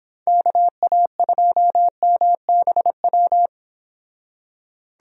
Morse code, heard as a keyed tone, about 29 wpm, 700 Hz. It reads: KA2MBW